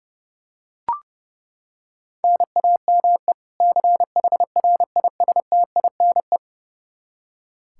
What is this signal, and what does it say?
Morse code, heard as a keyed tone, about 30 wpm, 700 Hz. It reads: NAME CHRISTINE
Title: Practice QSO - Exchange 1 - Beginning - 30wpm